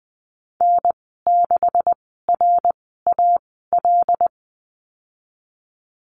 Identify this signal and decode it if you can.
Morse code, heard as a keyed tone, about 19 wpm, spaced as for 14 wpm, 700 Hz. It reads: N6RAL
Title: 186 Lesson 30g - Letter b - Call Signs 20wpm